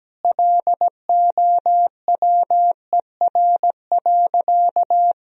Morse code, heard as a keyed tone, about 17 wpm, 700 Hz. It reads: LOWER.
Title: Sentences from Top 1000 Words - Review Part 4 - 17wpm